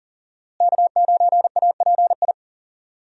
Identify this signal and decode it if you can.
Morse code, heard as a keyed tone, about 37 wpm, 700 Hz. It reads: K9API
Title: USA Call Signs - 5 Characters - Repeated 3x - 40wpm